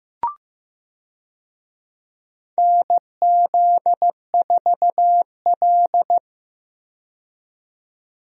Morse code, heard as a keyed tone, about 15 wpm, 700 Hz. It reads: NZ4L
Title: USA Call Signs - 4 Characters - 15wpm